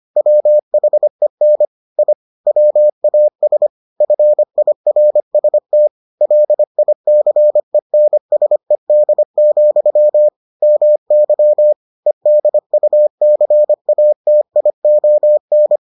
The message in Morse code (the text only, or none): WHEN I WAS FIRST LICENSED, MY EDUCATION